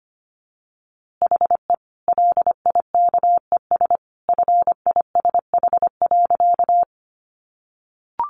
Morse code, heard as a keyed tone, about 25 wpm, 700 Hz. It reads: HE LIKES FISH.
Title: Easy Phrases 25wpm